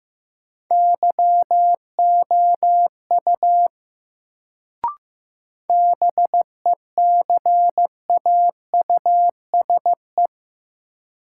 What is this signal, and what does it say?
Morse code, heard as a keyed tone, about 15 wpm, 700 Hz. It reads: YOU BECAUSE
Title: Sets of 2 Words - New Word Drill - 15wpm